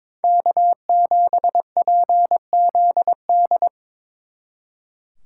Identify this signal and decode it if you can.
Morse code, heard as a keyed tone, about 26 wpm, 700 Hz. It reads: K7PZD
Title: Speed Racing - USA 1x3 Call Signs 22wpm